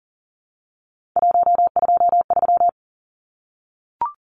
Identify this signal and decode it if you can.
Morse code, heard as a keyed tone, about 40 wpm, 700 Hz. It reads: 123
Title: Numbers - 3-Digit 40wpm